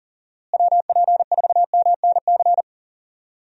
Morse code, heard as a keyed tone, about 40 wpm, 700 Hz. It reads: WP4MNC
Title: USA Call Signs - 6 Characters - Repeated 2x - 40wpm